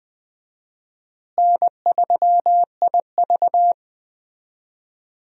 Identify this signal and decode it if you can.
Morse code, heard as a keyed tone, about 20 wpm, 700 Hz. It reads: N3IV
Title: USA Call Signs - 4 Characters - Repeated 2x - 20wpm